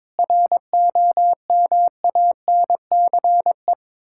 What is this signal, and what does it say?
Morse code, heard as a keyed tone, about 22 wpm, 690 Hz. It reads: ROMANCE
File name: Difficult Words with Hints - 22wpm